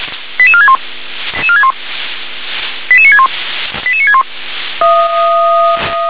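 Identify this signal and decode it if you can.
Five-tone selective call (ZVEI), heard as quick runs of five tones, tones of about 70 ms. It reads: 88461 88961 88961
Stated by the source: Five-tone ZVEI selective call received on SDR.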